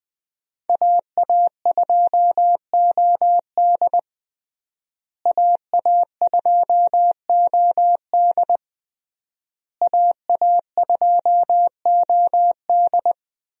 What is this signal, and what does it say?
Morse code, heard as a keyed tone, about 21 wpm, 700 Hz. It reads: AA2OD AA2OD AA2OD
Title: USA Call Signs - 5 Characters - Repeated 3x - 20wpm